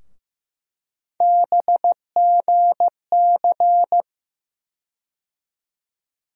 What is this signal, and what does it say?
Morse code, heard as a keyed tone, about 15 wpm, 700 Hz. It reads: BGC
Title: USA Call Sign Postfixes - 3 Letters 15wpm